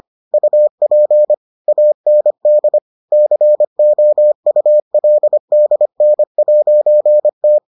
Morse code, heard as a keyed tone, about 25 wpm, 600 Hz. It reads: UP AND COULDN'T